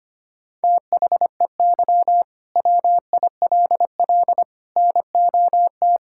Morse code, heard as a keyed tone, about 25 wpm, 700 Hz. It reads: THEY WILL NOT
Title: Sentences from Top 1000 Words - Review Part 5 - 25wpm